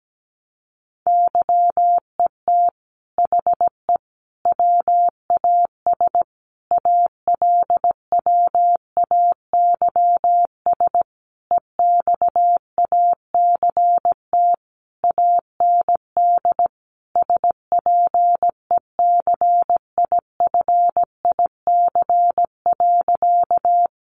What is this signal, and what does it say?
Morse code, heard as a keyed tone, about 17 wpm, 700 Hz. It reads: YET HE WAS ALWAYS EXACT AND SPECIFIC.
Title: Sentences from Top 1300 Words 17wpm